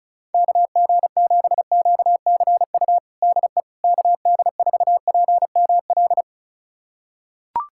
Morse code, heard as a keyed tone, about 36 wpm, 700 Hz. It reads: KG7QCU DE KD4PML
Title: USA Callsign DE Callsign - 6 Characters - 35wpm